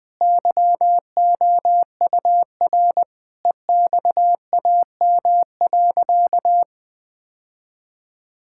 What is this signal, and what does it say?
Morse code, heard as a keyed tone, about 20 wpm, 700 Hz. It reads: YOUR EXAM.
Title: Sentences from Top 1000 Words - Review Part 5 - 20wpm